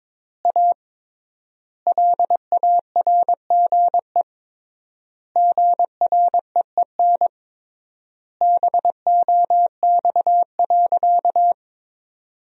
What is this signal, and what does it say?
Morse code, heard as a keyed tone, about 22 wpm, 700 Hz. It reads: A LARGE GREEN BOX.
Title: 3x Word-Spacing - Sentences from Top 500 Words - 22wpm